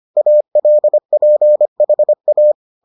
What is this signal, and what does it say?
Morse code, heard as a keyed tone, about 25 wpm, 600 Hz. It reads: ALPHA